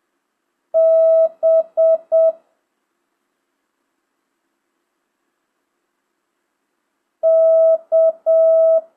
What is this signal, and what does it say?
Morse code, heard as a keyed tone, about 7 wpm, 650 Hz. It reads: B K
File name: jh8cbh / 2025cw